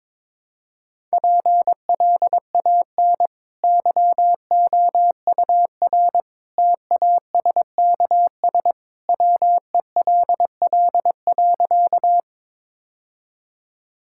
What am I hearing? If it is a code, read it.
Morse code, heard as a keyed tone, about 22 wpm, 700 Hz. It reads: PLAN YOUR TASKS WELL.
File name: practice / 22wpm / Easy Sentences with Hints - 22wpm